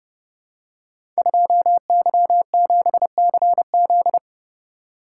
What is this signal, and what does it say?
Morse code, heard as a keyed tone, about 29 wpm, 700 Hz. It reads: 2Y7CZ
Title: Code Groups - 5 Letter:Numbers - 30wpm